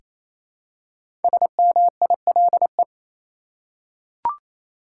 Morse code, heard as a keyed tone, about 28 wpm, 700 Hz. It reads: SMILE
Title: Five Letter Words - 28wpm